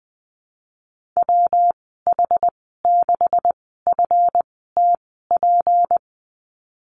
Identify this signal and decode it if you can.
Morse code, heard as a keyed tone, about 19 wpm, spaced as for 14 wpm, 700 Hz. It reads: WH6FTP
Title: 173 Lesson 28g - Number 9 - Call Signs 20wpm